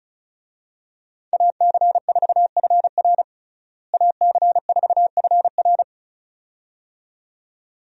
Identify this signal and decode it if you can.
Morse code, heard as a keyed tone, about 36 wpm, 700 Hz. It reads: AC4FR AC4FR
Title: USA Call Signs - 5 Characters - Repeated 2x - 35wpm